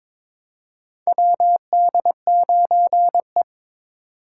Morse code, heard as a keyed tone, about 22 wpm, 700 Hz. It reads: WD9E
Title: USA Call Signs - 4 Characters - 3x Character Spacing - 22wpm